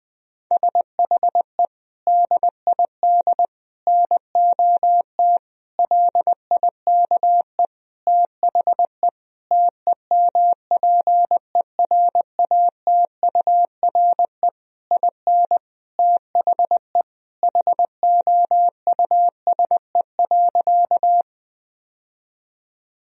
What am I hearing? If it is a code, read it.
Morse code, heard as a keyed tone, about 20 wpm, 700 Hz. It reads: SHE DID NOT LIKE THE TEMPERATURE IN THE HOUSE.
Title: Sentences from Top 1000 Words - Review Part 4 - 20wpm